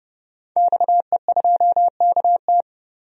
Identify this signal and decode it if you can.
Morse code, heard as a keyed tone, about 32 wpm, 700 Hz. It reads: XE2KT
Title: DXCC-100 Call Signs - 5 Characters - 30wpm